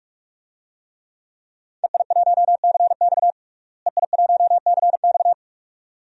Morse code, heard as a keyed tone, about 41 wpm, 700 Hz. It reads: EI1CX EI1CX
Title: DXCC-100 Call Signs - 5 Characters - Repeated 2x - 45wpm